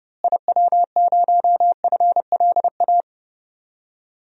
Morse code, heard as a keyed tone, about 30 wpm, 700 Hz. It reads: IW0FLA
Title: DXCC-100 Call Signs - 6 Characters - Repeated 3x - 30wpm